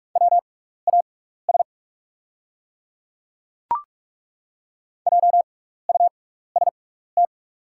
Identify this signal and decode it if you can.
Morse code, heard as a keyed tone, about 42 wpm, spaced as for 15 wpm, 690 Hz. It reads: WAS JUST
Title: 6x Character Spacing - Top 300 Words - 45wpm